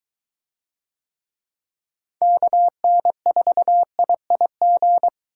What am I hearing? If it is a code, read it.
Morse code, heard as a keyed tone, about 23 wpm, 700 Hz. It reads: KN4IIG